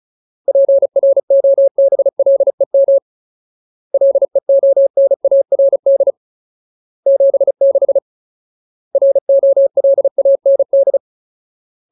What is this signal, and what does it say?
Morse code, heard as a keyed tone, about 36 wpm, 550 Hz. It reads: PROBLEM LEONARD 76 ROLAND
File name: Ultimate Word List / 35WPM Ultimate